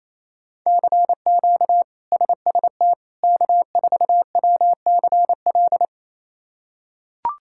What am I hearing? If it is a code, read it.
Morse code, heard as a keyed tone, about 28 wpm, 700 Hz. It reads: CQ SST K4WCL
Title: K1USN SST QSO Practice - 28wpm